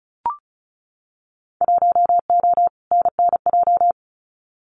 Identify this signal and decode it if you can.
Morse code, heard as a keyed tone, about 36 wpm, 700 Hz. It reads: 1O NNJ
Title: Winter Field Day Exchanges 35wpm